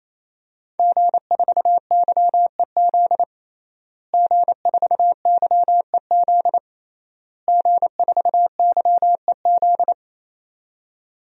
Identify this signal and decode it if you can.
Morse code, heard as a keyed tone, about 28 wpm, 700 Hz. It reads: G4YEZ G4YEZ G4YEZ
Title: DXCC-100 Call Signs - 5 Characters - Repeated 3x - 28wpm